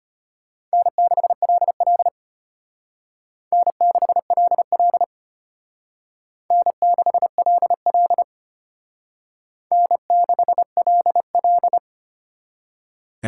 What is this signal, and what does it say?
Morse code, heard as a keyed tone, about 32 wpm, 700 Hz. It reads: N6LL N6LL N6LL N6LL